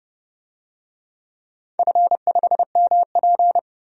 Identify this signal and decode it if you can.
Morse code, heard as a keyed tone, about 30 wpm, 700 Hz. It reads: F5MP
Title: DXCC-100 Call Signs - 4 Characters - 30wpm